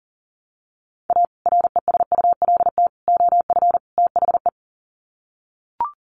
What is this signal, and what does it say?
Morse code, heard as a keyed tone, about 40 wpm, 700 Hz. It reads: A RESULT OF THE
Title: Sets of 4 Words 40wpm